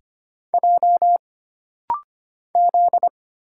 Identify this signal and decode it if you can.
Morse code, heard as a keyed tone, about 25 wpm, 700 Hz. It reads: J Z
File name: Single Character (Letter:Number:Punctuation:Prosign) - Rapid-Fire - 25wpm